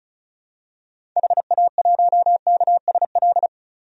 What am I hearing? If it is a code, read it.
Morse code, heard as a keyed tone, about 34 wpm, 700 Hz. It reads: HA1KSL
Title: DXCC-100 Call Signs - 6 Characters - 35wpm